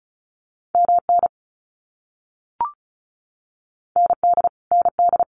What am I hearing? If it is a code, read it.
Morse code, heard as a keyed tone, about 34 wpm, 700 Hz. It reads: MN ND ND
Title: US States Repeated 3x 35wpm